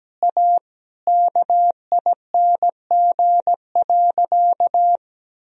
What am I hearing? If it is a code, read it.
Morse code, heard as a keyed tone, about 17 wpm, 700 Hz. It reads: A KING.
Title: Sentences from Top 1000 Words - Review Part 3 - 17wpm